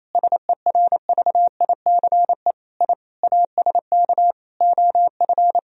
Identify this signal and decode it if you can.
Morse code, heard as a keyed tone, about 28 wpm, 700 Hz. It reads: SERVICE I ASK OF
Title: Sentences from Top 1000 Words - Review Part 4 - 28wpm